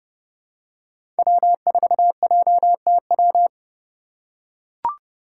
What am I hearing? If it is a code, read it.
Morse code, heard as a keyed tone, about 30 wpm, 700 Hz. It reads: W4JTW
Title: USA Call Signs - 5 Characters - 2x Character Spacing - 30wpm